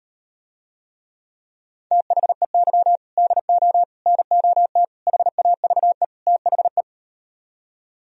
Morse code, heard as a keyed tone, about 38 wpm, 700 Hz. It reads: THEY DO NOT HAVE THE